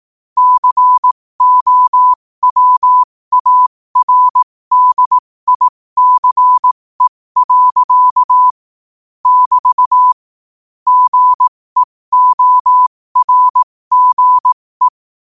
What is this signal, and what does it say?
Morse code, heard as a keyed tone, about 18 wpm, spaced as for 15 wpm, 1000 Hz. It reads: COWARDICE. <BT> GEORGE
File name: qotdfiles / 15wpm